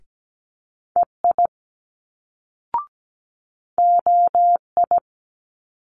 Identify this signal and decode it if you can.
Morse code, heard as a keyed tone, about 17 wpm, 700 Hz. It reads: EI OI
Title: Multi-letter Phonemes Quick Fire 17wpm